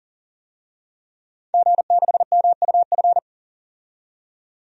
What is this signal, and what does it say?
Morse code, heard as a keyed tone, about 40 wpm, 700 Hz. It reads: G6MUF
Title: DXCC-100 Call Signs - 5 Characters - 40wpm